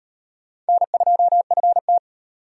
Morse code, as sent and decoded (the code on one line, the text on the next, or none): -. ..--- ..-. -
N2FT